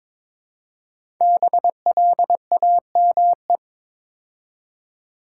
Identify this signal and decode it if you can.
Morse code, heard as a keyed tone, about 22 wpm, 700 Hz. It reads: BLAME
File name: 5 Letter Words - First 3 Letters Given - 22wpm